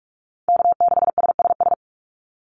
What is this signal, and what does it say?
Morse code, heard as a keyed tone, about 44 wpm, 700 Hz. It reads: K6SSS
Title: Call Signs - 45wpm